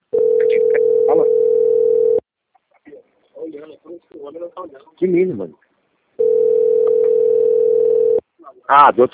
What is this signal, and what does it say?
Ringback tone (North America), 440 + 480 Hz, with speech.
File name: out- - - - - .37932